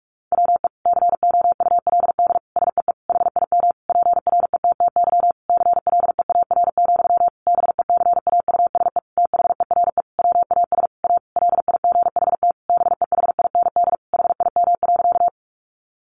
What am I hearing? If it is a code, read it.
Morse code, heard as a keyed tone, about 44 wpm, 700 Hz. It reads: WE COULD SEE HIM PRETTY CLEAR, BECAUSE THERE WAS A LIGHT BEHIND HIM.
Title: Sentences from Top 1000 Words 45wpm